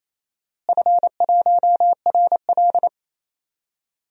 Morse code, heard as a keyed tone, about 28 wpm, 700 Hz. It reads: F1RL
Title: DXCC-100 Call Signs - 4 Characters - Repeated 2x - 28wpm